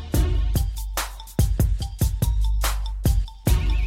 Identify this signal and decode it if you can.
There is no code.